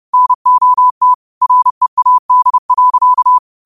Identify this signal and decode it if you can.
Morse code, heard as a keyed tone, about 30 wpm, 1000 Hz. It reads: NOT READ.
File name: qotdfiles / 30wpm